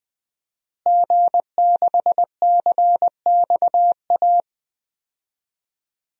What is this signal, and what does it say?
Morse code, heard as a keyed tone, about 20 wpm, 700 Hz. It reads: G6CXA
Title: DXCC-100 Call Signs - 5 Characters - Repeated 2x - 20wpm